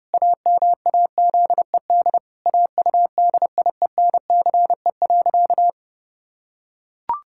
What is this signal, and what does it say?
Morse code, heard as a keyed tone, about 30 wpm, 700 Hz. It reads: AMAZED AUDIENCE.
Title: Pangram Challenge v2 - 30wpm